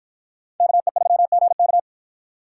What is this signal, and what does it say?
Morse code, heard as a keyed tone, about 51 wpm, 700 Hz. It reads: KE3GK